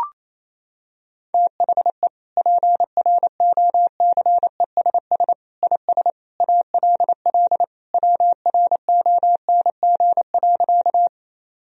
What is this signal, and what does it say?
Morse code, heard as a keyed tone, about 28 wpm, 700 Hz. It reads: THE PROCESS IS ALL WRONG.
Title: Sentences from Top 1000 Words - Review Part 3 - 28wpm